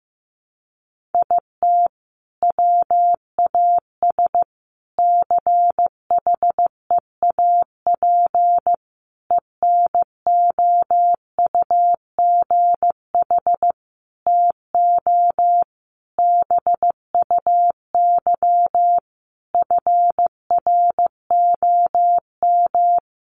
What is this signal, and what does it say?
Morse code, heard as a keyed tone, about 15 wpm, 700 Hz. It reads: IT WAS CHEAP ENOUGH TO BUY FROM
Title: Sentences from Top 1200 Words 15wpm